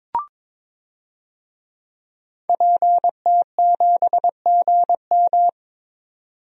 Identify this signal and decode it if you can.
Morse code, heard as a keyed tone, about 21 wpm, 700 Hz. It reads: PT7GM
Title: DXCC-100 Call Signs - 5 Characters - 22wpm